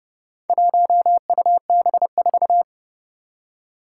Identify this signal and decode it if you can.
Morse code, heard as a keyed tone, about 30 wpm, 700 Hz. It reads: 1UB4
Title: Four Challenging Letter-Number Combinations 30wpm